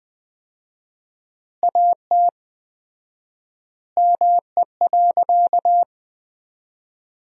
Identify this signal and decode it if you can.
Morse code, heard as a keyed tone, about 20 wpm, 700 Hz. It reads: AT ME.
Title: 4x Word-Spacing - Sentences from Top 200 Words - 20wpm